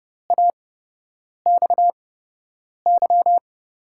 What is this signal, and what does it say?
Morse code, heard as a keyed tone, about 30 wpm, spaced as for 8 wpm, 700 Hz. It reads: AXY
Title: USA Call Signs - 6 Characters - 8x Character Spacing - 30wpm